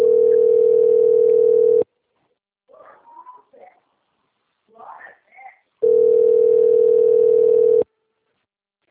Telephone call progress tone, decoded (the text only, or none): ringback tone